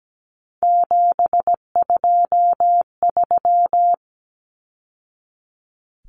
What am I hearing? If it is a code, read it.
Morse code, heard as a keyed tone, about 17 wpm, 700 Hz. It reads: 723
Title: Numbers - 3-Digit 17wpm